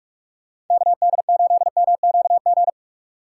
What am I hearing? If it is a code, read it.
Morse code, heard as a keyed tone, about 43 wpm, 700 Hz. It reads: KD8MQG